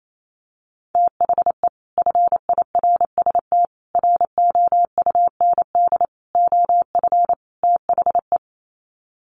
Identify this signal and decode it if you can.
Morse code, heard as a keyed tone, about 28 wpm, 700 Hz. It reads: THE FIRST ROUND OF THE
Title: Sets of 5 Words Encore 28wpm